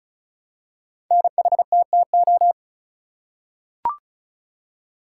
Morse code, heard as a keyed tone, about 36 wpm, 700 Hz. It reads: NHTTO
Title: Code Groups - 5 Letters - 35wpm